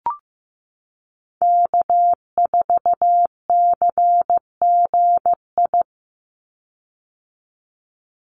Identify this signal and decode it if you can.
Morse code, heard as a keyed tone, about 15 wpm, 700 Hz. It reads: K4CGI
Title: USA Call Signs 15wpm